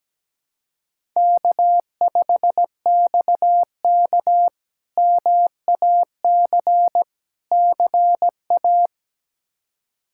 Morse code, heard as a keyed tone, about 17 wpm, 700 Hz. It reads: K5XK MAC CA
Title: K1USN SST QSO Practice - 17wpm